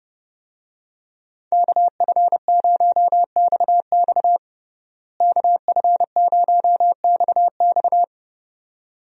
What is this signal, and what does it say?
Morse code, heard as a keyed tone, about 30 wpm, 700 Hz. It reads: KF0XX KF0XX
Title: USA Call Signs - 5 Characters - Repeated 2x - 30wpm